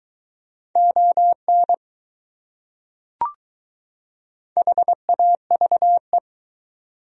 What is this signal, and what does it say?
Morse code, heard as a keyed tone, about 21 wpm, 700 Hz. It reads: ON HAVE